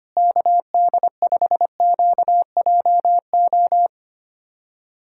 Morse code, heard as a keyed tone, about 25 wpm, 700 Hz. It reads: KD5QJO
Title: USA Callsign DE Callsign 25wpm